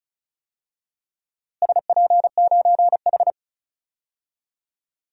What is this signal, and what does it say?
Morse code, heard as a keyed tone, about 34 wpm, 700 Hz. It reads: SP9H
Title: DXCC-100 Call Signs - 4 Characters - Repeated 3x - 35wpm